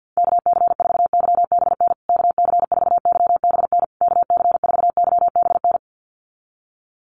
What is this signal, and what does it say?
Morse code, heard as a keyed tone, about 48 wpm, 700 Hz. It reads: KC4YBN KC4YBN KC4YBN
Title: USA Call Signs Repeated 3x 50wpm